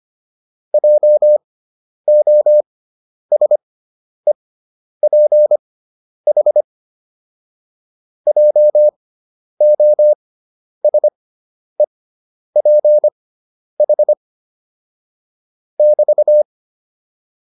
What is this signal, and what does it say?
Morse code, heard as a keyed tone, about 25 wpm, spaced as for 10 wpm, 600 Hz. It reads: JOSEPH JOSEPH <BT>
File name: QSO301 10